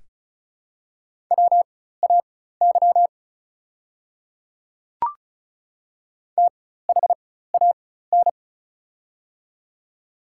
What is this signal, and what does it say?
Morse code, heard as a keyed tone, about 36 wpm, spaced as for 16 wpm, 700 Hz. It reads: WAY THAN
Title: 4x Character Spacing - Top 100 Words - 35wpm